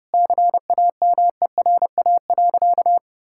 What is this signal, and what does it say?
Morse code, heard as a keyed tone, about 30 wpm, 700 Hz. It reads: CAMERA.
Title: Sentences from Top 1000 Words - Review Part 5 - 30wpm